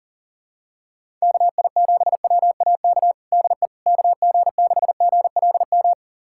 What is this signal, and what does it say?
Morse code, heard as a keyed tone, about 40 wpm, 700 Hz. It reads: KI7WAK DE KG6GLM
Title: USA Callsign DE Callsign - 6 Characters - 40wpm